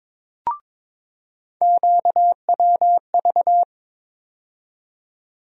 Morse code, heard as a keyed tone, about 22 wpm, 700 Hz. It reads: QWV
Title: USA Call Sign Postfixes - 3 Letters 22wpm